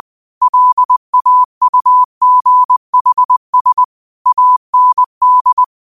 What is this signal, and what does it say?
Morse code, heard as a keyed tone, about 20 wpm, 1000 Hz. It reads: LAUGHS AND
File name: qotdfiles / 20wpm